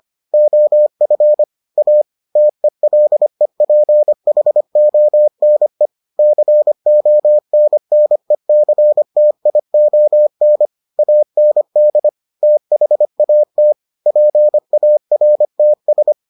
Morse code, heard as a keyed tone, about 25 wpm, 600 Hz. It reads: OF A TELEPHONE CONNECTION AND THAT PARTS